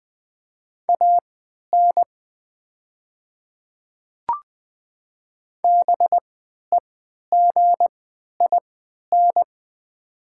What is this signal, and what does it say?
Morse code, heard as a keyed tone, about 20 wpm, spaced as for 11 wpm, 700 Hz. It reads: AN BEGIN
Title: 3x Character Spacing - Top 500 Words - 20wpm